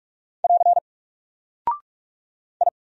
Morse code, heard as a keyed tone, about 44 wpm, 700 Hz. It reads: <AR> I
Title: Single Character (Letter:Number:Punctuation:Prosign) Alt 45wpm